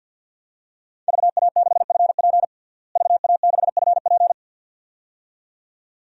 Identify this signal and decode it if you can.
Morse code, heard as a keyed tone, about 52 wpm, 700 Hz. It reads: VA6FP VA6FP
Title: DXCC-100 Call Signs - 5 Characters - Repeated 2x - 50wpm